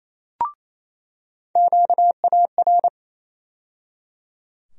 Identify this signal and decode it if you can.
Morse code, heard as a keyed tone, about 28 wpm, 700 Hz. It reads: QAR
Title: USA Call Sign Postfixes - 3 Letters 28wpm